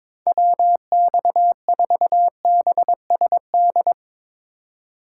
Morse code, heard as a keyed tone, about 22 wpm, 700 Hz. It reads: WX4BSD
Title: Speed Racing - USA 2x3 Call Signs 22wpm